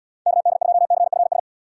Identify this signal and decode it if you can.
Morse code, heard as a keyed tone, about 76 wpm, 700 Hz. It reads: KD2CFS